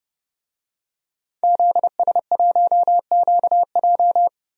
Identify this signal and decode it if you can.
Morse code, heard as a keyed tone, about 30 wpm, 700 Hz. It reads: ZS1QJ
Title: DXCC-100 Call Signs - 5 Characters - 30wpm